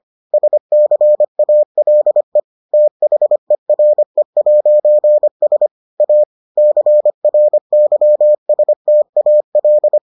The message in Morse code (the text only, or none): SCALE THERE'S A CRYSTAL